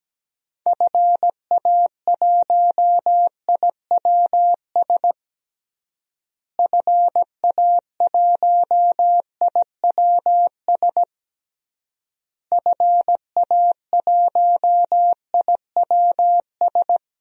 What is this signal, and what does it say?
Morse code, heard as a keyed tone, about 17 wpm, 700 Hz. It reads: FA1IWS FA1IWS FA1IWS
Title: DXCC-100 Call Signs - 6 Characters - Repeated 3x - 17wpm